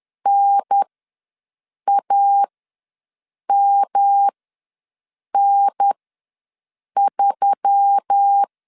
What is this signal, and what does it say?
Morse code, heard as a keyed tone, about 10 wpm, spaced as for 6 wpm, 800 Hz. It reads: NAMN3
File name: LEZIONI / 5 LBVC / 5 L ws 10wpm 3636